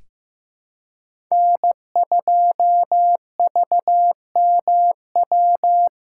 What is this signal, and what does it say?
Morse code, heard as a keyed tone, about 15 wpm, 700 Hz. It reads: N2VMW
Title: USA Callsign DE Callsign 15wpm